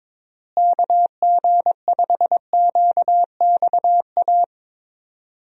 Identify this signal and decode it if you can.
Morse code, heard as a keyed tone, about 23 wpm, 700 Hz. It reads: KG5QXA
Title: USA Call Signs - 6 Characters - Repeated 3x - 22wpm